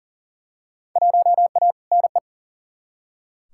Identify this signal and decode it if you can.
Morse code, heard as a keyed tone, about 40 wpm, 700 Hz. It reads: 1A NE
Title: Field Day Exchanges Encore - 40wpm